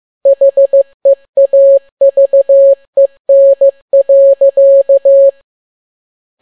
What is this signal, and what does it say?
Morse code, heard as a keyed tone, about 15 wpm, 550 Hz. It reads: HEAVEN.
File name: n6tt p1 15C13